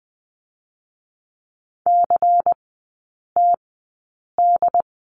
Morse code, heard as a keyed tone, about 20 wpm, spaced as for 8 wpm, 700 Hz. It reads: CTD
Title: 100 Lesson 17e - Letter u - Three Characters 20wpm